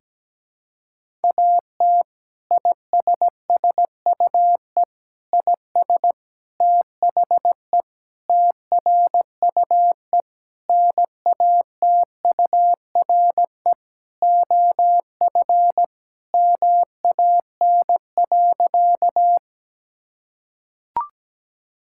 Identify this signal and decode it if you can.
Morse code, heard as a keyed tone, about 17 wpm, 700 Hz. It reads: AT ISSUE IS THE TRUE NATURE OF MAN.
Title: Sentences from Top 1000 Words - Review Part 6 - 17wpm